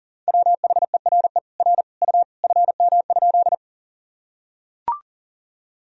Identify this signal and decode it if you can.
Morse code, heard as a keyed tone, about 40 wpm, 700 Hz. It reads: WHERE R U FM?
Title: Sentences from Top 200 Words and Abbreviations 40wpm